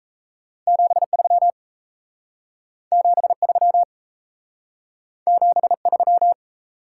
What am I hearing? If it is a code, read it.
Morse code, heard as a keyed tone, about 36 wpm, 700 Hz. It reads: 73 73 73